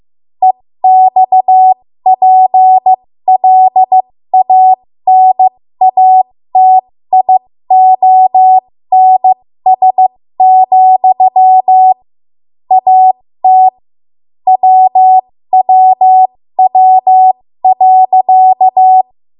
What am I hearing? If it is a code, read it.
Morse code, heard as a keyed tone, about 15 wpm, spaced as for 13 wpm, 750 Hz. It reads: EXPLANATIONS, AT WWW.
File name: Morse / 13WPM